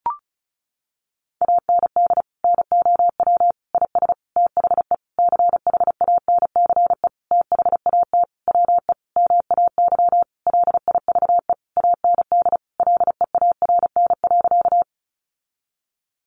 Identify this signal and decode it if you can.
Morse code, heard as a keyed tone, about 34 wpm, 700 Hz. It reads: AND NOW IS THE CHANCE THAT WE MAY LIVE AND LEARN.
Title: Sentences from Top 800 Words 35wpm